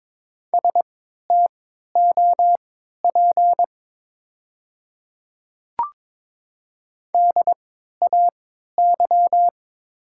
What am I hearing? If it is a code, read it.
Morse code, heard as a keyed tone, about 22 wpm, spaced as for 13 wpm, 690 Hz. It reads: STOP DAY
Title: 3x Character Spacing - Top 300 Words - 22wpm